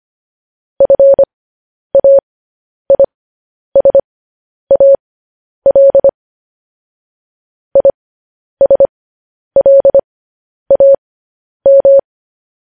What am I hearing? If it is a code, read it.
Morse code, heard as a keyed tone, about 25 wpm, spaced as for 10 wpm, 550 Hz. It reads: FAISAL ISLAM
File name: podcast / News headlines-1x-25x10